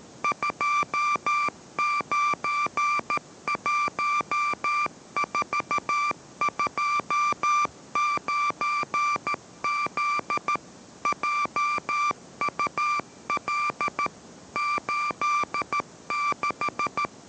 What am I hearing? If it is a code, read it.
Morse code, heard as a keyed tone, about 13 wpm, 1180 Hz, with noise. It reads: 291429Z JUL 86